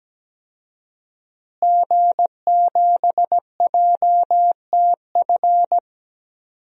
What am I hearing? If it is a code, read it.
Morse code, heard as a keyed tone, about 17 wpm, 700 Hz. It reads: G7JTF
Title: DXCC-100 Call Signs - 5 Characters - Repeated 3x - 17wpm